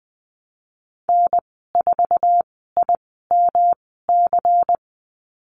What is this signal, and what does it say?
Morse code, heard as a keyed tone, about 20 wpm, spaced as for 15 wpm, 700 Hz. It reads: N4IMC
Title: 140 Lesson 23g - Letter f - Call Signs 20wpm